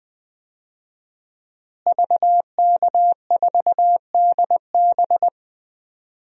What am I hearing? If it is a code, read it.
Morse code, heard as a keyed tone, about 20 wpm, 700 Hz. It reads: VK4DB
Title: DXCC-100 Call Signs - 5 Characters - Repeated 3x - 20wpm